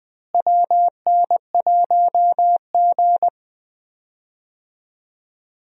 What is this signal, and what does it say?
Morse code, heard as a keyed tone, about 21 wpm, 700 Hz. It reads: WN1G
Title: USA Call Signs - 4 Characters - Repeated 3x - 20wpm